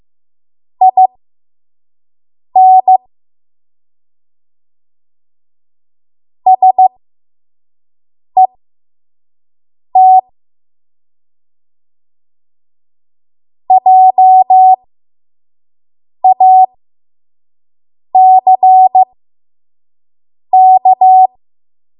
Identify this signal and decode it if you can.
Morse code, heard as a keyed tone, about 15 wpm, spaced as for 5 wpm, 750 Hz. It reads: IN SET JACK
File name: Morse / 05WPM